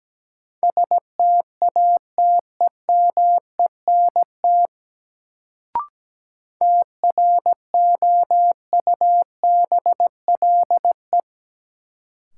Morse code, heard as a keyed tone, about 17 wpm, 700 Hz. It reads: STATEMENT TROUBLE
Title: Sentences from Top 1200 Words - New Word Drill - 17wpm